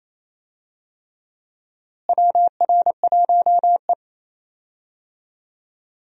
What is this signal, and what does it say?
Morse code, heard as a keyed tone, about 28 wpm, 700 Hz. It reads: WR1E
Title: USA Call Signs - 4 Characters - Repeated 3x - 28wpm